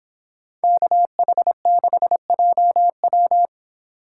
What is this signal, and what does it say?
Morse code, heard as a keyed tone, about 25 wpm, 700 Hz. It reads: KH6JW